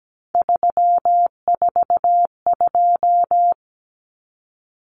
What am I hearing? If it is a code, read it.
Morse code, heard as a keyed tone, about 17 wpm, 700 Hz. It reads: 342
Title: Numbers - 3-Digit 17wpm